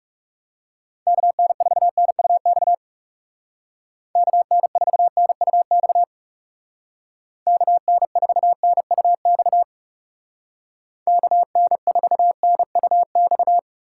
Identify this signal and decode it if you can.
Morse code, heard as a keyed tone, about 40 wpm, 700 Hz. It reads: KN4NUX KN4NUX KN4NUX KN4NUX